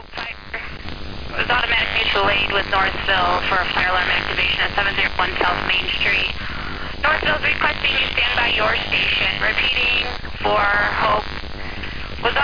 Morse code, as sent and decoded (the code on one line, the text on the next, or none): none